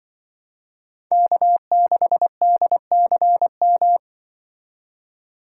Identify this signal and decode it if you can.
Morse code, heard as a keyed tone, about 24 wpm, 700 Hz. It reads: K6DCM
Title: Speed Racing - USA 1x3 Call Signs 20wpm